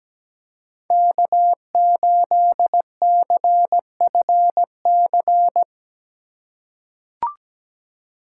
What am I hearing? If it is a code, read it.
Morse code, heard as a keyed tone, about 17 wpm, 700 Hz. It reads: K8CFC
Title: USA Call Signs - 5 Characters - 8x Character Spacing - 17wpm